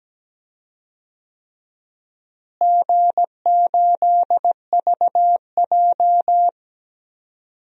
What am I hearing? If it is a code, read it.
Morse code, heard as a keyed tone, about 17 wpm, 700 Hz. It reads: G8VJ
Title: DXCC-100 Call Signs - 4 Characters - Repeated 3x - 17wpm